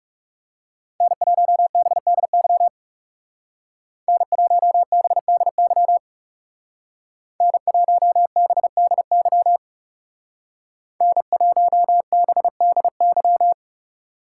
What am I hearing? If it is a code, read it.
Morse code, heard as a keyed tone, about 35 wpm, 700 Hz. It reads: N1BDY N1BDY N1BDY N1BDY
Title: Speed Racing - USA Call Signs 30wpm